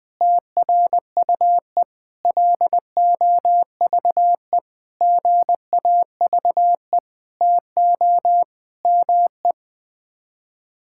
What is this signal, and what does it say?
Morse code, heard as a keyed tone, about 20 wpm, 700 Hz. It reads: TRUE LOVE GAVE TO ME
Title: Sets of 9 Words - 20wpm